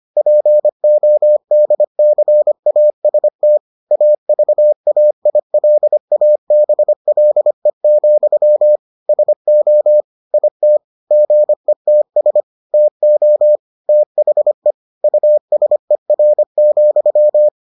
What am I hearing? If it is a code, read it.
Morse code, heard as a keyed tone, about 25 wpm, 600 Hz. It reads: PODCAST AVAILABLE, SO IT GETS TO THE USER,